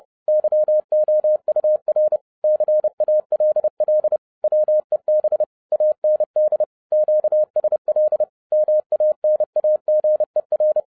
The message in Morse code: YOUR CALL WEB AND QSL MANAGER

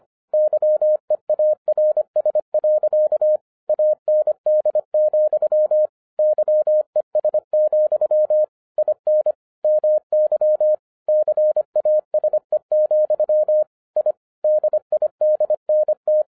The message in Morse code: YEARS. AND, YES, IN MY CASE, I DIDNT

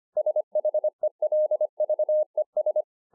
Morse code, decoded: SHELVES

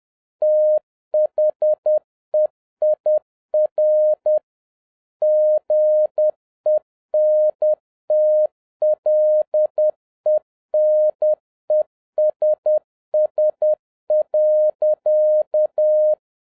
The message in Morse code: THEIR GENTLENESS.

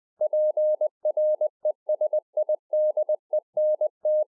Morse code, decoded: PRESIDENT

